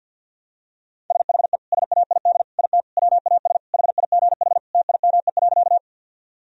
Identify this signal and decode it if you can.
Morse code, heard as a keyed tone, about 53 wpm, 700 Hz. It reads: SHE SAID IT WAS HIGH TIME.
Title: Sentences from Top 1000 Words - Review Part 1 - 50wpm